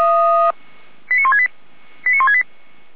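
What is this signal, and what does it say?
Five-tone selective call (ZVEI), heard as quick runs of five tones, tones of about 80 ms. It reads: 89168 89168
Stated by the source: Received on an SDR; ZVEI five-tone call 89095.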